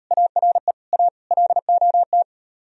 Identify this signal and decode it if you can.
Morse code, heard as a keyed tone, about 39 wpm, 700 Hz. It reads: ARE A LOT